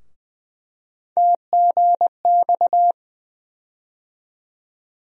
Morse code, heard as a keyed tone, about 20 wpm, 700 Hz. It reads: TGX
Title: USA Call Sign Postfixes - 3 Letters 20wpm